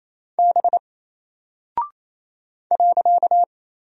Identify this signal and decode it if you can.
Morse code, heard as a keyed tone, about 28 wpm, 700 Hz. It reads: B .
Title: Single Character (Letter:Number:Punctuation:Prosign) 28wpm